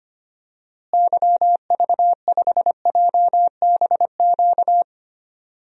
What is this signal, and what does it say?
Morse code, heard as a keyed tone, about 25 wpm, 700 Hz. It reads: YV5JBQ
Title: DXCC-100 Call Signs - 6 Characters - Repeated 2x - 25wpm